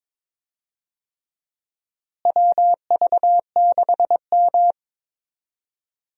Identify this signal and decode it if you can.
Morse code, heard as a keyed tone, about 22 wpm, 700 Hz. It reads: WV6M
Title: USA Call Signs - 4 Characters - Repeated 3x - 22wpm